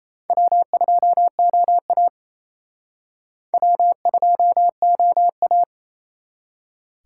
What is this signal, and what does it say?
Morse code, heard as a keyed tone, about 35 wpm, 700 Hz. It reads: W2OA W2OA
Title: Speed Racing - USA 1x2 Call Signs 28wpm